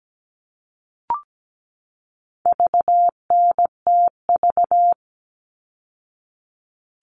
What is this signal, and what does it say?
Morse code, heard as a keyed tone, about 17 wpm, 700 Hz. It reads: VNTV
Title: All Cut Numbers - 4-Digit 17wpm